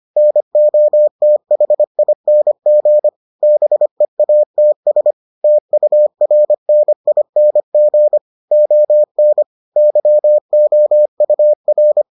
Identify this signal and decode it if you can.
Morse code, heard as a keyed tone, about 25 wpm, 600 Hz. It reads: NOTHING BEATS TURNING ON YOUR